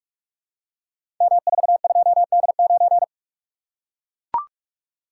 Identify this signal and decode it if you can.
Morse code, heard as a keyed tone, about 44 wpm, 690 Hz. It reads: M42D9
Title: Code Groups - 5 Letter:Numbers - 45wpm